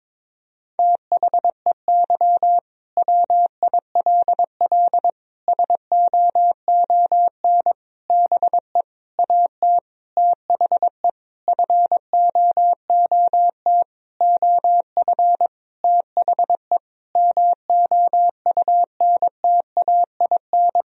Morse code, heard as a keyed tone, about 22 wpm, 700 Hz. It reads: THEY WILL SOON BE AT THE FOOT OF THE MOUNTAIN
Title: Sentences from Top 1000 Words - Review Part 4 - 22wpm